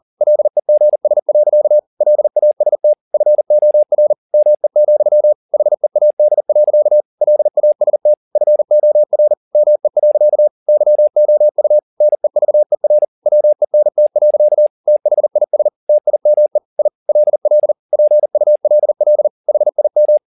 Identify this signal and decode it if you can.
Morse code, heard as a keyed tone, about 40 wpm, 600 Hz. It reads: LEGS. LAST FOR ME, HEAD. LAST FOR ME. YOU NEVER WENT. THIS TIME I LL PULL HIM
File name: the old man and the sea 2 40wpm